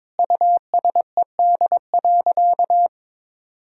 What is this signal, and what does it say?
Morse code, heard as a keyed tone, about 22 wpm, 700 Hz. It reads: USED.
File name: Sentences with 9-Word Phrases - 22wpm